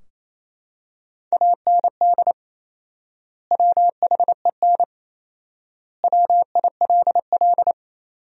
Morse code, heard as a keyed tone, about 28 wpm, 700 Hz. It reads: AND WHEN WILL
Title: 4x Word-Spacing - Sentences from Top 100 Words - 28wpm